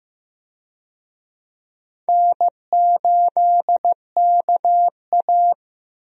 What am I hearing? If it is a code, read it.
Morse code, heard as a keyed tone, about 15 wpm, 700 Hz. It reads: N8KA
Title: USA Call Signs - 4 Characters - 15wpm